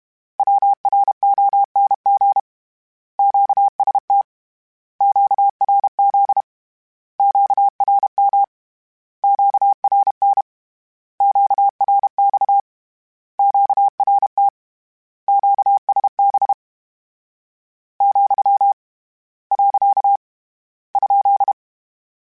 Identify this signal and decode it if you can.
Morse code, heard as a keyed tone, about 31 wpm, 800 Hz. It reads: WRONG QST QRZ QRM QRN QRX QRT QSB , . ?
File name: Short Runs / Short Runs-mp3s / Words @30